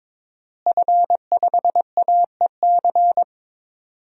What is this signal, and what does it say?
Morse code, heard as a keyed tone, about 22 wpm, 700 Hz. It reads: F5AEC
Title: DXCC-100 Call Signs - 5 Characters - Repeated 3x - 22wpm